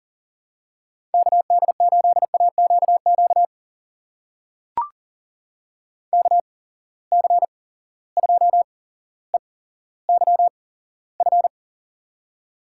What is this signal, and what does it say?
Morse code, heard as a keyed tone, about 40 wpm, spaced as for 17 wpm, 700 Hz. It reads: KD8AQQ KC2EYF
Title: USA Call Signs - 6 Characters - 8x Character Spacing - 40wpm